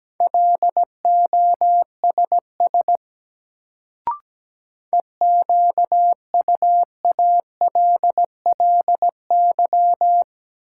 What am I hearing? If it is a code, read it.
Morse code, heard as a keyed tone, about 17 wpm, 700 Hz. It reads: LOSS EQUALLY
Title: Sentences from Top 1400 Words - New Word Drill - 17wpm